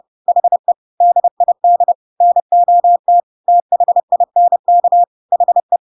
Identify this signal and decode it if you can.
Morse code, heard as a keyed tone, about 30 wpm, 700 Hz. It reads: HE DID NOT THINK HE
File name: Pinocchio / Pinocchio ch4 30 30 ik1hge